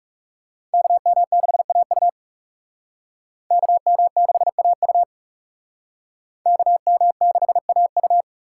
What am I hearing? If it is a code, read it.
Morse code, heard as a keyed tone, about 41 wpm, 700 Hz. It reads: KM6AU KM6AU KM6AU